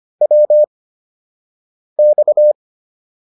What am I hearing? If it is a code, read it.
Morse code, heard as a keyed tone, about 25 wpm, spaced as for 6 wpm, 600 Hz. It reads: WX